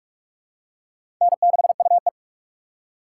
Morse code, heard as a keyed tone, about 43 wpm, 690 Hz. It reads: N6UE
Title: USA Call Signs - 4 Characters - 2x Character Spacing - 45wpm